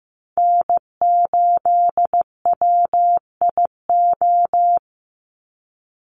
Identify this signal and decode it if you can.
Morse code, heard as a keyed tone, about 15 wpm, 700 Hz. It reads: N8WIO
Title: USA Call Signs 15wpm